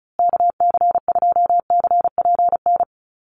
Morse code, heard as a keyed tone, about 34 wpm, 700 Hz. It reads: KC2CPN
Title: USA Call Signs Repeated 3x 35wpm